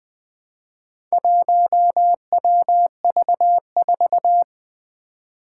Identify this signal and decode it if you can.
Morse code, heard as a keyed tone, about 20 wpm, 700 Hz. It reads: 1WV4
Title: Four Challenging Letter-Number Combinations 20wpm